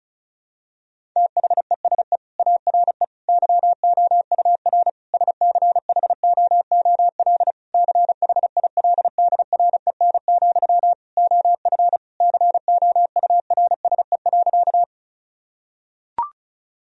Morse code, heard as a keyed tone, about 36 wpm, 700 Hz. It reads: THESE ARE YOUR SCHOOL CHILDREN, OF COURSE.
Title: Sentences from Top 1000 Words - Review Part 1 - 35wpm